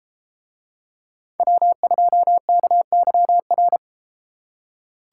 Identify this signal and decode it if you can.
Morse code, heard as a keyed tone, about 30 wpm, 700 Hz. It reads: W2KYR